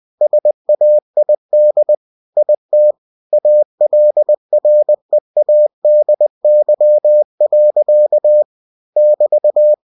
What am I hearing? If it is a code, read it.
Morse code, heard as a keyed tone, about 20 wpm, 600 Hz. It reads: SAID IT ALREADY. <BT>
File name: cw-books / winnie-the-pooh / winnie-20wpm-page-0057